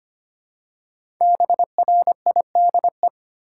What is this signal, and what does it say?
Morse code, heard as a keyed tone, about 25 wpm, 700 Hz. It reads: BRIDE
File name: 5 Letter Words - First 3 Letters Given - 25wpm